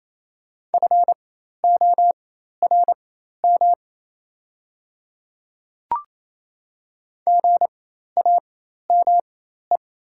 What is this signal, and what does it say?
Morse code, heard as a keyed tone, about 28 wpm, spaced as for 13 wpm, 700 Hz. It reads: FORM GAME
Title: 4x Character Spacing - Top 500 Words - 28wpm